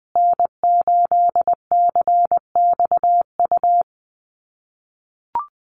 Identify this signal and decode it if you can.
Morse code, heard as a keyed tone, about 20 wpm, 700 Hz. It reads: N8CXU
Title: USA Call Signs Repeated 2x 20wpm